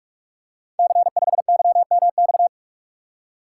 Morse code, heard as a keyed tone, about 44 wpm, 700 Hz. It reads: C5YMX
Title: Code Groups - 5 Letter:Numbers - 45wpm